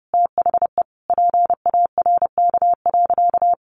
Morse code, heard as a keyed tone, about 30 wpm, 700 Hz. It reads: THE PARK.
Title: Sentences from Top 1300 Words 30wpm